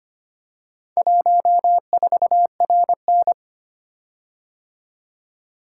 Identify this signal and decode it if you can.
Morse code, heard as a keyed tone, about 25 wpm, 700 Hz. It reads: 14RN
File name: practice / 25wpm / Four Letter-Number Combinations - 25wpm